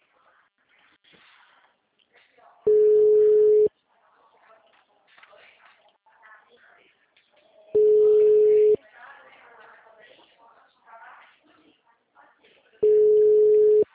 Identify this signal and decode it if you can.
Ringback tone (Europe, 425 Hz), with speech.